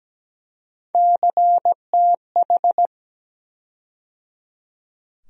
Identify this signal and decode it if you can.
Morse code, heard as a keyed tone, about 17 wpm, 700 Hz. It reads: CTH
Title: USA Call Sign Postfixes - 3 Letters 17wpm